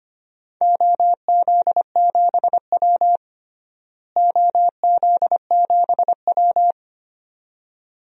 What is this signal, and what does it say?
Morse code, heard as a keyed tone, about 25 wpm, 700 Hz. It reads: OZ7W OZ7W
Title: DXCC-100 Call Signs - 4 Characters - Repeated 2x - 25wpm